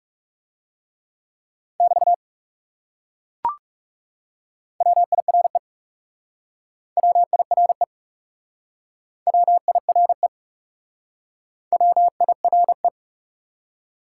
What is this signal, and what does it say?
Morse code, heard as a keyed tone, about 37 wpm, 700 Hz. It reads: <BT> WIRE WIRE WIRE W I R E
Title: Speed-Racing Top 180 QSO Elements 30wpm